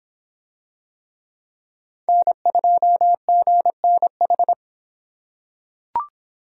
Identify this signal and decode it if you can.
Morse code, heard as a keyed tone, about 26 wpm, 700 Hz. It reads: N2GNH